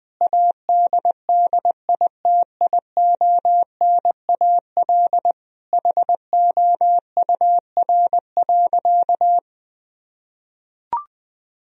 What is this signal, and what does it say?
Morse code, heard as a keyed tone, about 21 wpm, 700 Hz. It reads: ADDITIONAL HOUR.
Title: Sentences from Top 1000 Words - Review Part 4 - 20wpm